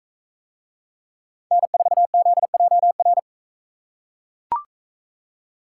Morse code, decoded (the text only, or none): N4ZJR